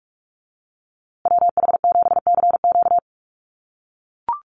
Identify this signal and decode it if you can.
Morse code, heard as a keyed tone, about 44 wpm, 700 Hz. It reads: WH7CQ
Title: USA Call Signs Repeated 2x 45wpm